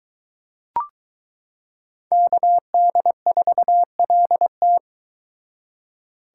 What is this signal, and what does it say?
Morse code, heard as a keyed tone, about 22 wpm, 700 Hz. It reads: KD4LT